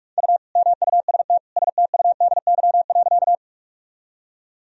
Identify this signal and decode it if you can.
Morse code, heard as a keyed tone, about 46 wpm, 700 Hz. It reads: U MUST STUDY.
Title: Sentences from Top 300 Words and Abbreviations 45wpm